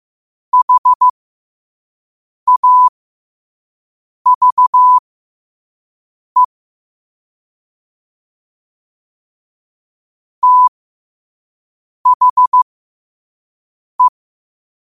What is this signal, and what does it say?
Morse code, heard as a keyed tone, about 15 wpm, spaced as for 5 wpm, 1000 Hz. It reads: HAVE THE
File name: qotdfiles / 5wpm